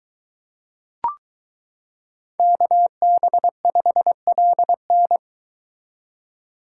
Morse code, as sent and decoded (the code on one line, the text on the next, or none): -.- -... ..... .-.. -.
KB5LN